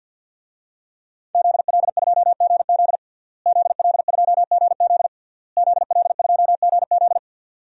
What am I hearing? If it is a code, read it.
Morse code, heard as a keyed tone, about 48 wpm, 700 Hz. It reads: ZL2GZ ZL2GZ ZL2GZ
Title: DXCC-100 Call Signs - 5 Characters - Repeated 3x - 50wpm